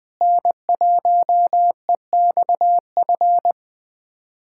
Morse code, heard as a keyed tone, about 21 wpm, 700 Hz. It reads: N1EXF
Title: K1USN SST QSO Practice - 20wpm